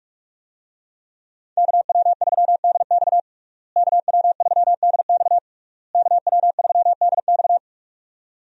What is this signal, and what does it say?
Morse code, heard as a keyed tone, about 46 wpm, 700 Hz. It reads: KW3DX KW3DX KW3DX
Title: USA Call Signs - 5 Characters - Repeated 3x - 45wpm